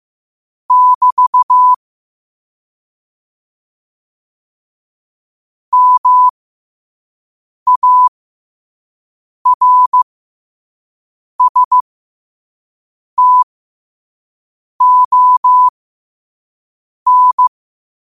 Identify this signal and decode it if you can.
Morse code, heard as a keyed tone, about 15 wpm, spaced as for 5 wpm, 1000 Hz. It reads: <BT> MARSTON
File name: qotdfiles / 5wpm